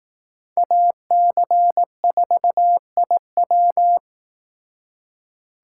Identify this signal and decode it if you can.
Morse code, heard as a keyed tone, about 18 wpm, 700 Hz. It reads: AC4IW
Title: Speed Racing - USA 2x2 Call Signs 15wpm